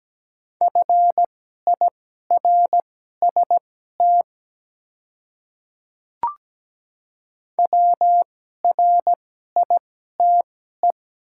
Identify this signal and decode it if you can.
Morse code, heard as a keyed tone, about 17 wpm, spaced as for 12 wpm, 700 Hz. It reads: FIRST WRITE
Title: 2x Character Spacing - Top 300 Words - 17wpm